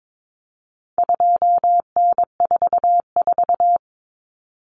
Morse code, heard as a keyed tone, about 22 wpm, 700 Hz. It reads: 2N44
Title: Common Cut Numbers - 4-Digit 22wpm